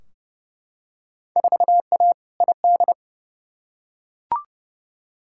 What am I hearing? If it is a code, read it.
Morse code, heard as a keyed tone, about 30 wpm, 700 Hz. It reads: 4A ID
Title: Field Day Exchanges Encore - 30wpm